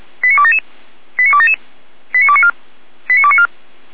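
Five-tone selective call (ZVEI), heard as quick runs of five tones, tones of about 70 ms. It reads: 89277 89277 89294 89294
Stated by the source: ZVEI five-tone call received on SDR.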